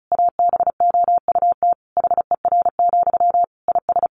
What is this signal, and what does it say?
Morse code, heard as a keyed tone, about 34 wpm, 700 Hz. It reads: ABOUT HER, IS
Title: Sentences from Top 1500 Words 35wpm